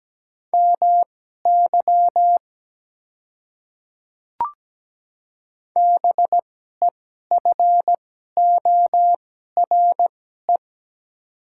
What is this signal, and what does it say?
Morse code, heard as a keyed tone, about 17 wpm, spaced as for 12 wpm, 700 Hz. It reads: MY BEFORE
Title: 2x Character Spacing - Top 200 Words - 17wpm